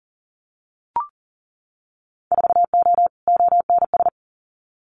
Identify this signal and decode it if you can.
Morse code, heard as a keyed tone, about 37 wpm, 700 Hz. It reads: 4O ONS
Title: Winter Field Day Exchanges 40wpm